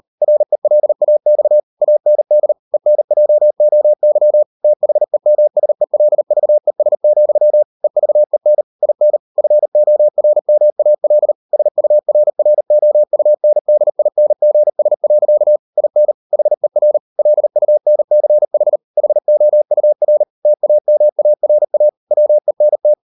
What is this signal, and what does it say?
Morse code, heard as a keyed tone, about 38 wpm, 600 Hz. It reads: RELAX AND ENJOY THEMSELVES, EVEN IN FORMAL SURROUNDINGS. IN HER LUNCH HOUR TAMARA WENT